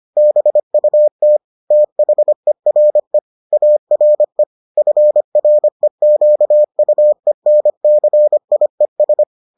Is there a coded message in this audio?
Yes: Morse code from 0.2 to 9.2 s, heard as a keyed tone, about 25 wpm, 600 Hz. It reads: BUT THERE ARE FREQUENCIES